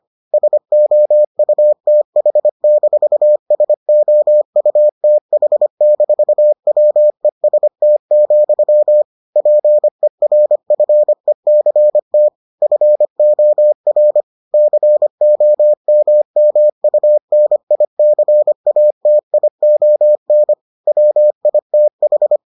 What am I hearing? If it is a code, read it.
Morse code, heard as a keyed tone, about 25 wpm, 600 Hz. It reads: SOUTH-SOUTH-WEST, PERFECT FOR COMMUNICATION WITH